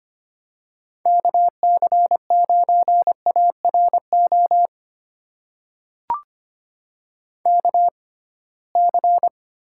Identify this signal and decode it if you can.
Morse code, heard as a keyed tone, about 25 wpm, spaced as for 19 wpm, 700 Hz. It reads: KC9ARO KC
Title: USA Call Signs - 6 Characters - 6x Character Spacing - 25wpm